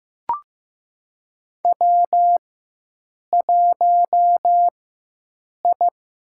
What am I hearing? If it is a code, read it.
Morse code, heard as a keyed tone, about 15 wpm, 700 Hz. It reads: W 1 I
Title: USA Call Signs - 4 Characters - 4x Character Spacing - 15wpm